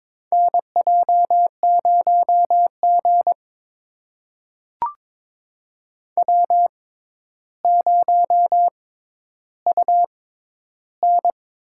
Morse code, heard as a keyed tone, about 22 wpm, spaced as for 11 wpm, 700 Hz. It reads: NJ0G W0UN
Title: USA Call Signs - 4 Characters - 6x Character Spacing - 22wpm